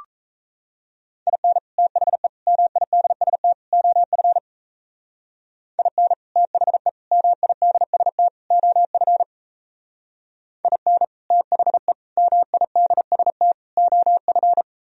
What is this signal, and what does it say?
Morse code, heard as a keyed tone, about 37 wpm, 700 Hz. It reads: IN THE MIDST OF IN THE MIDST OF IN THE MIDST OF